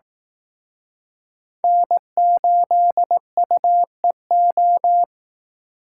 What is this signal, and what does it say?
Morse code, heard as a keyed tone, about 18 wpm, 700 Hz. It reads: N8UEO
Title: Speed Racing - USA 1x3 Call Signs 15wpm